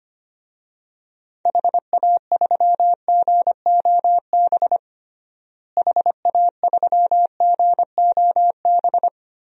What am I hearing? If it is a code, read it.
Morse code, heard as a keyed tone, about 25 wpm, 700 Hz. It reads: HA3GOB HA3GOB
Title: DXCC-100 Call Signs - 6 Characters - Repeated 3x - 25wpm